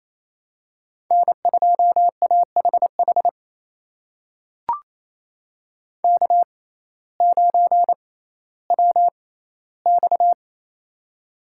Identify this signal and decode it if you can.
Morse code, heard as a keyed tone, about 28 wpm, spaced as for 15 wpm, 700 Hz. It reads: N2AHH K9WX
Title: USA Call Signs - 6x Character Spacing - 28wpm